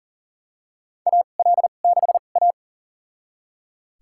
Morse code, heard as a keyed tone, about 40 wpm, spaced as for 29 wpm, 700 Hz. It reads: AL6A
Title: USA Call Signs - 4 Characters - 2x Character Spacing - 40wpm